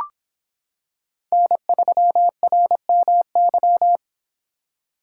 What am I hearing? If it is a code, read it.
Morse code, heard as a keyed tone, about 26 wpm, 700 Hz. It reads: N3RMY